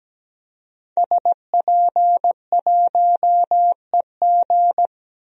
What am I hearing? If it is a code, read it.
Morse code, heard as a keyed tone, about 17 wpm, 700 Hz. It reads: SP1EG
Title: DXCC-100 Call Signs - 5 Characters - 17wpm